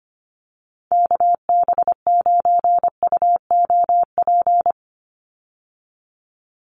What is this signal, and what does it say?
Morse code, heard as a keyed tone, about 25 wpm, 700 Hz. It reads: KB9UOP
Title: USA Call Signs 25wpm